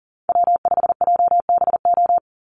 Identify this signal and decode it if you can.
Morse code, heard as a keyed tone, about 40 wpm, 700 Hz. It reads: W5JBO
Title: Call Signs Repeated 2x - 40wpm